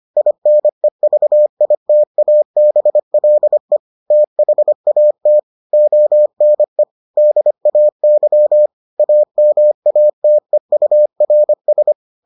Morse code, heard as a keyed tone, about 25 wpm, 600 Hz. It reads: INEVITABLE THAT ONE DAY AMATEURS